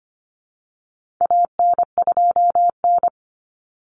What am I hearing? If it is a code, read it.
Morse code, heard as a keyed tone, about 25 wpm, 700 Hz. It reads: AN2N
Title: Common Cut Numbers - 4-Digit 25wpm